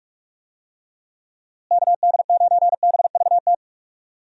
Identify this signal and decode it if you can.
Morse code, heard as a keyed tone, about 46 wpm, 700 Hz. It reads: KD9BVT
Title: USA Call Signs - 6 Characters - Repeated 2x - 45wpm